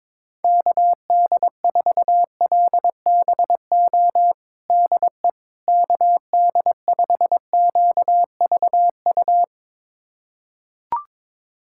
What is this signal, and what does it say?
Morse code, heard as a keyed tone, about 22 wpm, 700 Hz. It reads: KD4LBO DE KD5QVU
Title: USA Callsign DE Callsign - 6 Characters - 22wpm